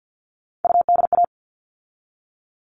Morse code, heard as a keyed tone, about 50 wpm, 700 Hz. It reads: UDA
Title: All Cut Numbers - 3-Digit 50wpm